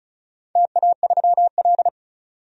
Morse code, heard as a keyed tone, about 35 wpm, 700 Hz. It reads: TA3L
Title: DXCC-100 Call Signs - 4 Characters - 35wpm